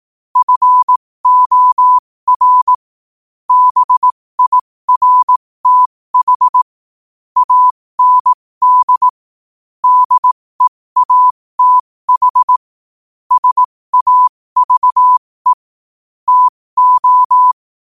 Morse code, heard as a keyed tone, about 18 wpm, spaced as for 15 wpm, 1000 Hz. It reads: FOR BIRTH AND DEATH SAVE TO
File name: qotdfiles / 15wpm